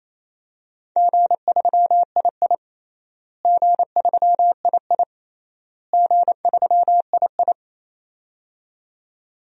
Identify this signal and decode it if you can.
Morse code, heard as a keyed tone, about 27 wpm, 700 Hz. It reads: G3II G3II G3II
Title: DXCC-100 Call Signs - 4 Characters - Repeated 3x - 28wpm